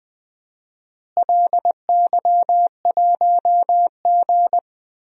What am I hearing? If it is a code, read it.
Morse code, heard as a keyed tone, about 20 wpm, 700 Hz. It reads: LY1G
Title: DXCC-100 Call Signs - 4 Characters - Repeated 3x - 20wpm